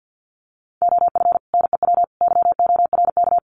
Morse code, heard as a keyed tone, about 50 wpm, 700 Hz. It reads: OF NEW YORK
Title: Sets of 3 Words 50wpm